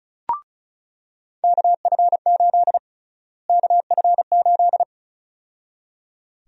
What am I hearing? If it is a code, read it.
Morse code, heard as a keyed tone, about 36 wpm, 700 Hz. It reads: KF8 KF8
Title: USA Call Sign Prefixes - Repeated 2x - 35wpm